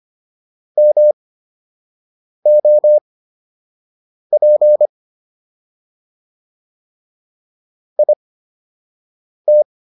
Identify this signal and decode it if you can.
Morse code, heard as a keyed tone, about 25 wpm, spaced as for 6 wpm, 600 Hz. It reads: MOP IT